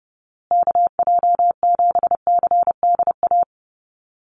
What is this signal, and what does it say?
Morse code, heard as a keyed tone, about 30 wpm, 700 Hz. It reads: KJ7CDA
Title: USA Call Signs Repeated 2x 30wpm